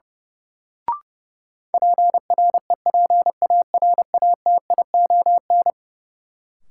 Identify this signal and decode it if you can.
Morse code, heard as a keyed tone, about 32 wpm, 700 Hz. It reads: PREPARATION
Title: Sentences from Top 1400 Words - New Word Drill - 30wpm